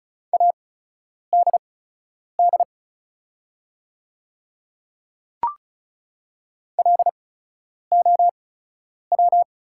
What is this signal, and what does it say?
Morse code, heard as a keyed tone, about 34 wpm, spaced as for 10 wpm, 700 Hz. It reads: ADD LOW
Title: 8x Character Spacing - Top 300 Words - 35wpm